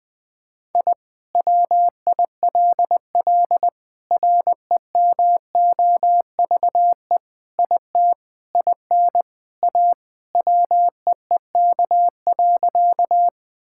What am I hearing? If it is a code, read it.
Morse code, heard as a keyed tone, about 20 wpm, 700 Hz. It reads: I WILL REMOVE IT IN A WEEK.
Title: Sentences from Top 1000 Words - Review Part 5 - 20wpm